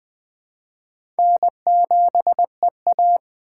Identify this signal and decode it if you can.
Morse code, heard as a keyed tone, about 20 wpm, 700 Hz. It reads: N7EA
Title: USA Call Signs - 4 Characters - 2x Character Spacing - 20wpm